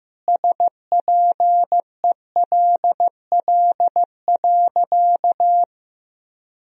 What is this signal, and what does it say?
Morse code, heard as a keyed tone, about 15 wpm, 700 Hz. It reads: SPELL.
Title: Sentences from Top 1000 Words - Review Part 1 - 15wpm